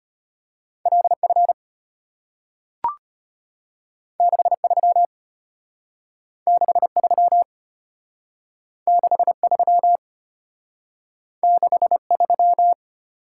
Morse code, heard as a keyed tone, about 34 wpm, 700 Hz. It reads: LF 63 63 63 63